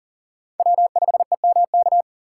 Morse code, heard as a keyed tone, about 37 wpm, 700 Hz. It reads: W5EMK
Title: USA Call Signs - 5 Characters - 6x Character Spacing - 40wpm